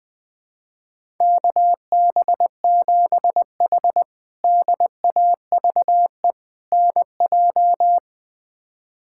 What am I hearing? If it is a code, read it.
Morse code, heard as a keyed tone, about 21 wpm, 700 Hz. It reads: KB7H DAVE NJ
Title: K1USN SST QSO Practice - 20wpm